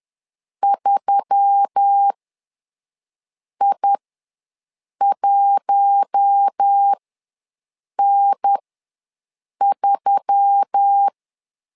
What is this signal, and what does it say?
Morse code, heard as a keyed tone, about 10 wpm, spaced as for 5 wpm, 800 Hz. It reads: 3I1N3
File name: LEZIONI / 4 Q1S3 ws 10wpm 2437